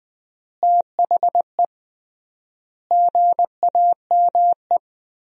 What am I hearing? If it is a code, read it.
Morse code, heard as a keyed tone, about 21 wpm, 700 Hz. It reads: THE GAME
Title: 3x Word-Spacing - Sentences from Top 500 Words - 20wpm